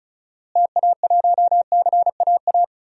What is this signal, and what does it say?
Morse code, heard as a keyed tone, about 36 wpm, 700 Hz. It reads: TA1CAA
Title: DXCC-100 Call Signs - 6 Characters - Repeated 2x - 35wpm